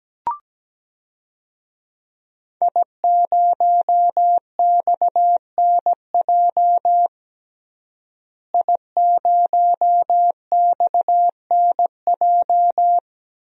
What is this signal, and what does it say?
Morse code, heard as a keyed tone, about 17 wpm, 700 Hz. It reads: I0XNJ I0XNJ
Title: DXCC-100 Call Signs - 5 Characters - Repeated 2x - 17wpm